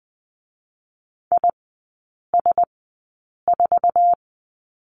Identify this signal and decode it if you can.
Morse code, heard as a keyed tone, about 20 wpm, spaced as for 8 wpm, 700 Hz. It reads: IS4
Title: 047 Lesson 9e - Number 4 - Three Characters 20wpm